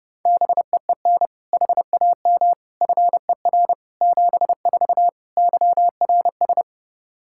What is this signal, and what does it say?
Morse code, heard as a keyed tone, about 30 wpm, 700 Hz. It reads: BEEN HAM FER 74 YRS
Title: Practice QSO - Exchange 3 - More Info - 30wpm